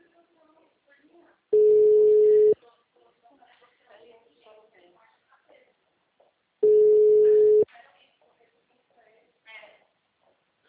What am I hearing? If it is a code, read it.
Ringback tone (Europe, 425 Hz), with speech.